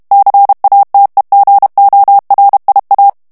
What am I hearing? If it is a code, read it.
Morse code, heard as a keyed tone, about 32 wpm, 800 Hz. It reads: CATEGORIA
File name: ITALIEN texte clair 2 vitesse 30 mots minute